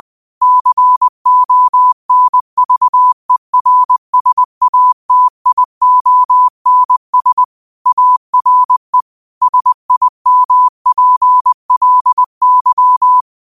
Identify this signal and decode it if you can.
Morse code, heard as a keyed tone, about 20 wpm, 1000 Hz. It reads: CONVERSATIONS ARE SIMPLY